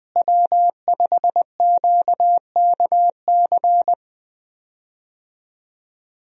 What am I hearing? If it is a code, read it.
Morse code, heard as a keyed tone, about 20 wpm, 700 Hz. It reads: W5QKC
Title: USA Call Signs - 5 Characters - Repeated 3x - 20wpm